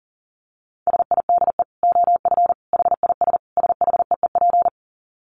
Morse code, heard as a keyed tone, about 40 wpm, 700 Hz. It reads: SIDE OF HIS SHEEP
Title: Dit Challenge 40wpm